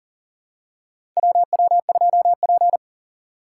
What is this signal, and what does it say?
Morse code, heard as a keyed tone, about 40 wpm, 700 Hz. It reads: WW2P
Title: K1USN SST QSO Practice - 40wpm